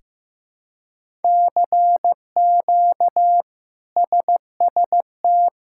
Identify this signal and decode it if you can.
Morse code, heard as a keyed tone, about 15 wpm, 700 Hz. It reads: CQ SST
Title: K1USN SST QSO Practice - 15wpm